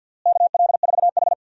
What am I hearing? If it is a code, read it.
Morse code, heard as a keyed tone, about 52 wpm, 700 Hz. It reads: KL4H
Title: USA Call Signs - 4 Characters - Repeated 3x - 50wpm